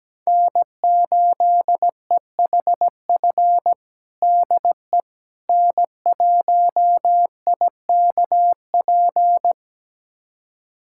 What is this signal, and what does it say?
Morse code, heard as a keyed tone, about 17 wpm, 700 Hz. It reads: N8EHF DE N1IKP
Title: USA Callsign DE Callsign - 5 Characters - 17wpm